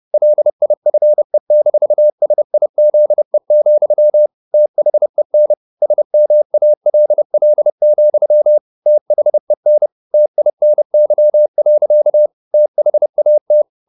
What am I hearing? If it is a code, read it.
Morse code, heard as a keyed tone, about 30 wpm, 600 Hz. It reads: LIFE-SIZE, THEN SMALL, THEN TINY. THAT